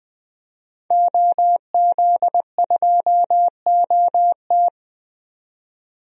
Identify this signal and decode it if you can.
Morse code, heard as a keyed tone, about 20 wpm, 700 Hz. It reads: OZ2OT
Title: DXCC-100 Call Signs - 5 Characters - 20wpm